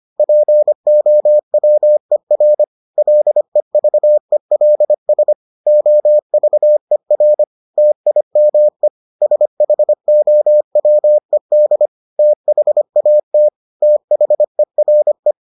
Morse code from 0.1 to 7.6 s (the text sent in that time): POWER LEVELS OVER